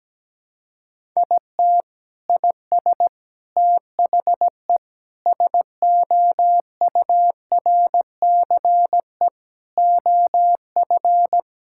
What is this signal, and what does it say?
Morse code, heard as a keyed tone, about 17 wpm, 700 Hz. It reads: IT IS THE SOURCE OF
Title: Sentences from Top 1000 Words - Review Part 5 - 17wpm